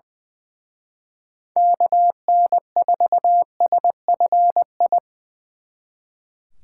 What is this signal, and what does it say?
Morse code, heard as a keyed tone, about 24 wpm, 700 Hz. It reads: KN4SFI
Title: Speed Racing - USA 2x3 Call Signs 20wpm